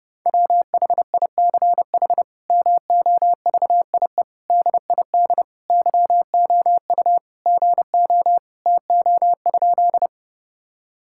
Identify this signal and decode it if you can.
Morse code, heard as a keyed tone, about 30 wpm, 700 Hz. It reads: WHICH MOVIE DID YOU GO TO?
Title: Sentences from Top 1000 Words - Review Part 5 - 30wpm